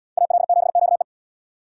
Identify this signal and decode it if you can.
Morse code, heard as a keyed tone, about 74 wpm, 700 Hz. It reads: AL7QE